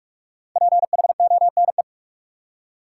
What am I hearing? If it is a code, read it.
Morse code, heard as a keyed tone, about 46 wpm, 700 Hz. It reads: PHONE
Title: Five Letter Words - 45wpm